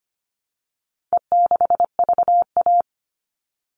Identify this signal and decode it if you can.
Morse code, heard as a keyed tone, about 25 wpm, 700 Hz. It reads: E6VA
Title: All Cut Numbers - 4-Digit 25wpm